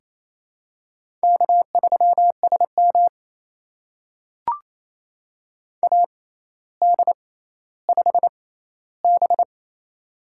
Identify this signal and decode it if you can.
Morse code, heard as a keyed tone, about 28 wpm, spaced as for 14 wpm, 700 Hz. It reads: K3SM AD5B
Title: USA Call Signs - 4 Characters - 6x Character Spacing - 28wpm